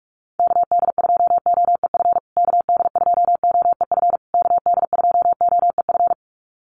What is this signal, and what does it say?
Morse code, heard as a keyed tone, about 44 wpm, 700 Hz. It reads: KD2OEF KD2OEF KD2OEF
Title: USA Call Signs Repeated 3x 45wpm